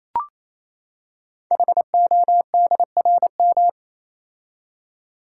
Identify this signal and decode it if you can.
Morse code, heard as a keyed tone, about 29 wpm, 700 Hz. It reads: HODRM
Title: Code Groups - 5 Letters - 28wpm